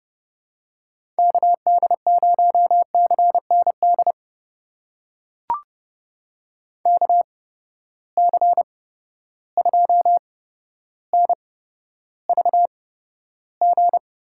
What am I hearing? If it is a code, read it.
Morse code, heard as a keyed tone, about 30 wpm, spaced as for 13 wpm, 700 Hz. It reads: KD0CND KC2NVG
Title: USA Call Signs - 6 Characters - 8x Character Spacing - 30wpm